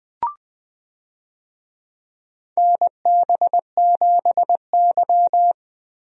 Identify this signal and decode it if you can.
Morse code, heard as a keyed tone, about 20 wpm, 700 Hz. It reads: NB7Y
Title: USA Call Signs - 4 Characters - Repeated 2x - 20wpm